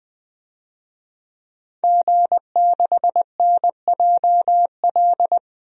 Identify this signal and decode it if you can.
Morse code, heard as a keyed tone, about 20 wpm, 700 Hz. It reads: G6NJL
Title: DXCC-100 Call Signs - 5 Characters - 20wpm